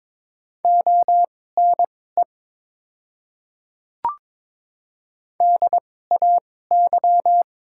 Morse code, heard as a keyed tone, about 22 wpm, spaced as for 16 wpm, 690 Hz. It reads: ONE DAY
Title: 2x Character Spacing - Top 100 Words - 22wpm